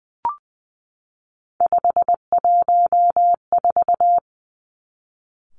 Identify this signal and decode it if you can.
Morse code, heard as a keyed tone, about 20 wpm, 700 Hz. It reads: 514
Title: Three Challenging Letter-Number Combinations 20wpm